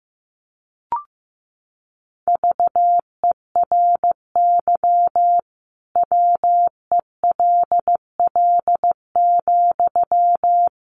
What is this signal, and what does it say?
Morse code, heard as a keyed tone, about 15 wpm, 700 Hz. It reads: VERY WELL,
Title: Sentences from Top 500 Words 15wpm